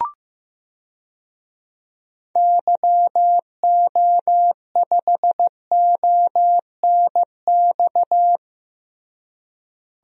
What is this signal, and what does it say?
Morse code, heard as a keyed tone, about 16 wpm, 700 Hz. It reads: YO5ONX
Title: DXCC-100 Call Signs - 6 Characters - Repeated 2x - 15wpm